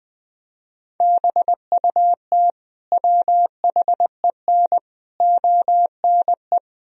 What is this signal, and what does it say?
Morse code, heard as a keyed tone, about 20 wpm, 700 Hz. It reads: BUT WHEN ONE
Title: Sentences from Top 1000 Words - Review Part 1 - 20wpm